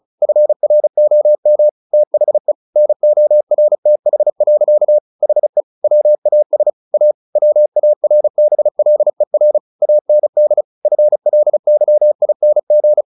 Morse code, heard as a keyed tone, about 34 wpm, 600 Hz. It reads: FROM THE NORTH. HE WAS A WARBLER AND FLYING
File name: the old man and the sea / the old man and the sea 1 35wpm